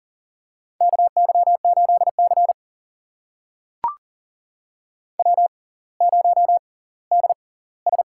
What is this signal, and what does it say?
Morse code, heard as a keyed tone, about 40 wpm, spaced as for 20 wpm, 700 Hz. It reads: KY8C W0DS
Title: USA Call Signs - 4 Characters - 6x Character Spacing - 40wpm